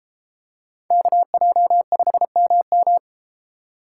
Morse code, heard as a keyed tone, about 32 wpm, 700 Hz. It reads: KJ5MM